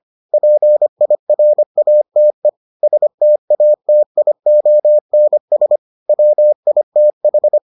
Morse code, heard as a keyed tone, about 25 wpm, 600 Hz. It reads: PIRATE STATIONS WITH